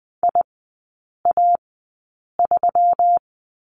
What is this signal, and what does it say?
Morse code, heard as a keyed tone, about 20 wpm, spaced as for 8 wpm, 700 Hz. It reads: IA3
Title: 172 Lesson 28e - Number 9 - Three Characters 20wpm